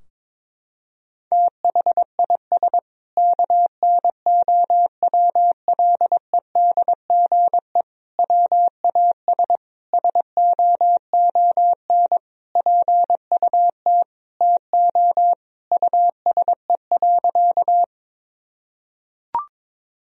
Morse code, heard as a keyed tone, about 21 wpm, 700 Hz. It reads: THIS KNOWLEDGE WAS SOON PUT TO USE.
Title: Sentences from Top 1000 Words - Review Part 3 - 22wpm